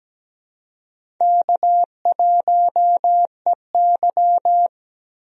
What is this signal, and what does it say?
Morse code, heard as a keyed tone, about 17 wpm, 700 Hz. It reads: K1EY
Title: USA Call Signs - 4 Characters - Repeated 2x - 17wpm